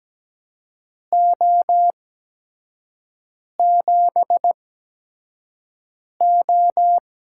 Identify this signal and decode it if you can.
Morse code, heard as a keyed tone, about 17 wpm, spaced as for 5 wpm, 700 Hz. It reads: O7O
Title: USA Call Signs - 4 Characters - 8x Character Spacing - 17wpm